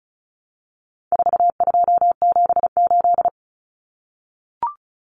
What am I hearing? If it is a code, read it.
Morse code, heard as a keyed tone, about 34 wpm, 700 Hz. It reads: 4278
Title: Numbers - 4-Digit 35wpm